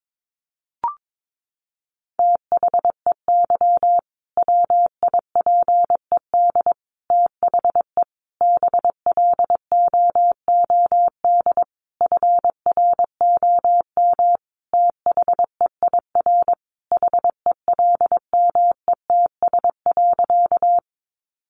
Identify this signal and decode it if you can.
Morse code, heard as a keyed tone, about 22 wpm, 700 Hz. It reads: THEY WIPED THE BLOOD FROM THEIR HELMETS.
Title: A Sound Of Thunder 22wpm